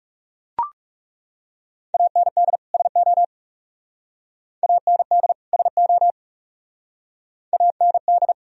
Morse code, heard as a keyed tone, about 38 wpm, 700 Hz. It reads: AND SO AND SO AND
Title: Speed-Racing - Sets of 2 Words 30wpm